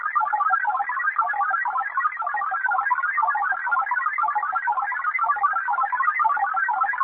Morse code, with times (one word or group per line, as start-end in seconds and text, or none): none